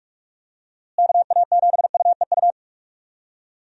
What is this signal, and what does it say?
Morse code, heard as a keyed tone, about 46 wpm, 700 Hz. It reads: KA7UEU
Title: USA Call Signs - 6 Characters - Repeated 2x - 45wpm